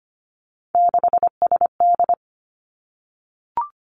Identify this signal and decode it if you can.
Morse code, heard as a keyed tone, about 25 wpm, 700 Hz. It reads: 6SD
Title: Three Challenging Letter-Number Combinations 25wpm